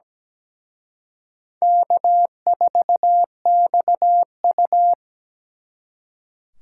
Morse code, heard as a keyed tone, about 20 wpm, 700 Hz. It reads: K4XU
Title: Speed Racing - USA Call Signs 17wpm